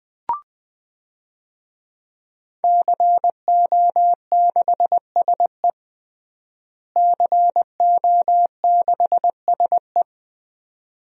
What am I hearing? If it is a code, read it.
Morse code, heard as a keyed tone, about 21 wpm, 700 Hz. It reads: CO6SE CO6SE
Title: DXCC-100 Call Signs - 5 Characters - Repeated 2x - 20wpm